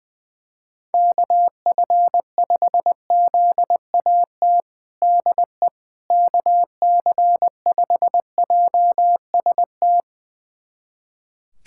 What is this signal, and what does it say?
Morse code, heard as a keyed tone, about 21 wpm, 700 Hz. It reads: KF5ZAT DE KC5JST
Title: USA Callsign DE Callsign - 6 Characters - 20wpm